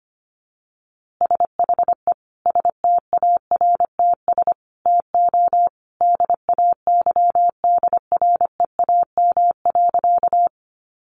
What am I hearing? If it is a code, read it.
Morse code, heard as a keyed tone, about 25 wpm, 700 Hz. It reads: SHE STARTS TO DAYDREAM.
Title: Easy Phrases 25wpm